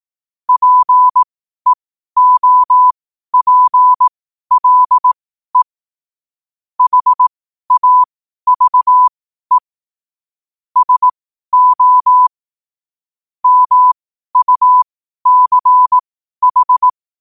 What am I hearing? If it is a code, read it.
Morse code, heard as a keyed tone, about 18 wpm, spaced as for 12 wpm, 1000 Hz. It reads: PEOPLE HAVE SO MUCH
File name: qotdfiles / 12wpm